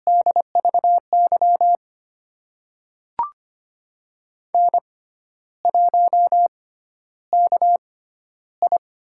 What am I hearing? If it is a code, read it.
Morse code, heard as a keyed tone, about 25 wpm, spaced as for 12 wpm, 700 Hz. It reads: DVY N1KI
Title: USA Call Signs - 5 Characters - 6x Character Spacing - 25wpm